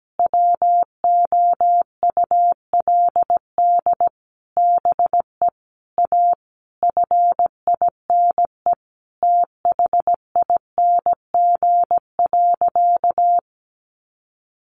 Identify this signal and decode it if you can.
Morse code, heard as a keyed tone, about 17 wpm, 700 Hz. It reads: WOULD BE A FINE THING.
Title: Sentences from Top 500 Words 17wpm